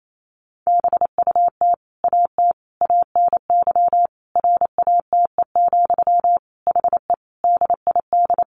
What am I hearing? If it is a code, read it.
Morse code, heard as a keyed tone, about 28 wpm, 700 Hz. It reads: BUT AT ANY RATE, HE DID
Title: Sentences from Top 700 Words 28wpm